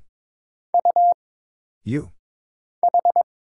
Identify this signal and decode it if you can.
Morse code, heard as a keyed tone, about 22 wpm, 700 Hz. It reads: U H
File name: Single Character (Letter:Number:Punctuation:Prosign) - Mind-Melt - 22wpm